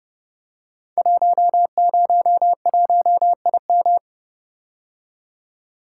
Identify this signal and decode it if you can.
Morse code, heard as a keyed tone, about 31 wpm, 700 Hz. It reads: 101IM
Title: Code Groups - 5 Letter:Numbers - 30wpm